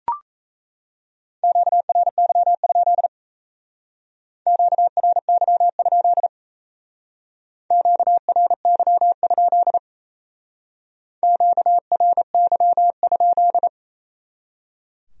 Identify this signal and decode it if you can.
Morse code, heard as a keyed tone, about 33 wpm, 700 Hz. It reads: QRY? QRY? QRY? QRY?
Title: Speed-Racing Q-Codes - 28wpm